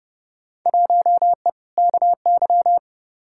Morse code, heard as a keyed tone, about 30 wpm, 700 Hz. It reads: 1E KY
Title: Field Day Exchanges - 30wpm